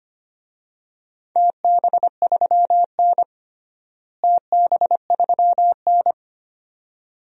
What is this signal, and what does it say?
Morse code, heard as a keyed tone, about 25 wpm, 700 Hz. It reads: TB3N TB3N
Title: DXCC-100 Call Signs - 4 Characters - Repeated 2x - 25wpm